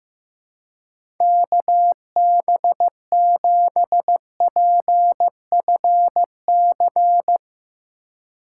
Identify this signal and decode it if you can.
Morse code, heard as a keyed tone, about 15 wpm, 700 Hz. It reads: KB7PFC
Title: USA Call Signs - 6 Characters - 4x Character Spacing - 15wpm